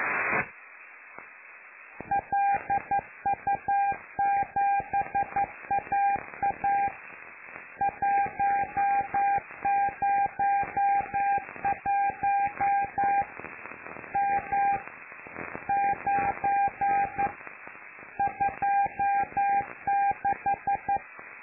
Morse code, heard as a keyed tone, about 11 wpm, 800 Hz. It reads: LU7AA 101 M 9 26
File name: lu4aao / Globo Amsat